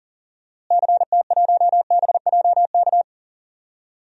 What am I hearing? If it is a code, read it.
Morse code, heard as a keyed tone, about 40 wpm, 700 Hz. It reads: CT1BJK
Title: DXCC-100 Call Signs - 6 Characters - Repeated 2x - 40wpm